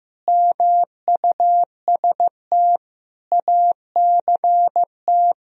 Morse code, heard as a keyed tone, about 15 wpm, 700 Hz. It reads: MUST ACT
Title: Sentences from Top 1000 Words - Review Part 4 - 15wpm